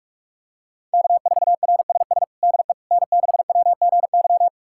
Morse code, heard as a keyed tone, about 44 wpm, 700 Hz. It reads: K4RSS DE N6WGY
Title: USA Callsign DE Callsign 45wpm